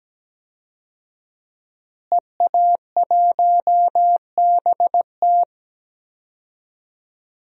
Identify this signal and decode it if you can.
Morse code, heard as a keyed tone, about 17 wpm, 700 Hz. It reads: EA1BT
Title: DXCC-100 Call Signs - 5 Characters - Repeated 2x - 17wpm